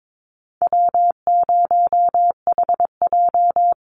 Morse code, heard as a keyed tone, about 22 wpm, 700 Hz. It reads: W0HJ
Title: USA Call Signs 22wpm